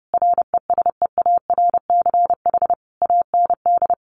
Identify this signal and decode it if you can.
Morse code, heard as a keyed tone, about 30 wpm, 700 Hz. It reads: RESEARCH AND
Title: Sentences from Top 1400 Words 30wpm